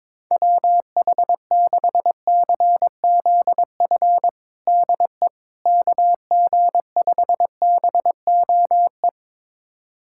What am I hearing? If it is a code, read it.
Morse code, heard as a keyed tone, about 22 wpm, 700 Hz. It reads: WH6CZF DE KG5BOE
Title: USA Callsign DE Callsign - 6 Characters - 22wpm